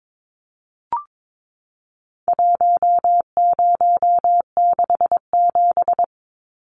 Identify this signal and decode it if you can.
Morse code, heard as a keyed tone, about 22 wpm, 700 Hz. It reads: 1067
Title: Numbers - 4-Digit 22wpm